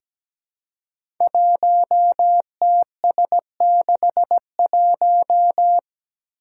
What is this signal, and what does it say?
Morse code, heard as a keyed tone, about 17 wpm, 700 Hz. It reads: 1TS61
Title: Five Challenging Letter-Number Combinations 17wpm